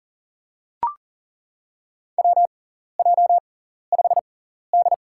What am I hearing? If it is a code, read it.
Morse code, heard as a keyed tone, about 40 wpm, spaced as for 14 wpm, 700 Hz. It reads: WJ5D
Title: USA Call Signs - 4 Characters - 6x Character Spacing - 40wpm